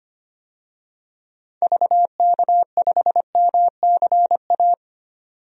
Morse code, heard as a keyed tone, about 25 wpm, 700 Hz. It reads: VK5MCA
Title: DXCC-100 Call Signs - 6 Characters - 25wpm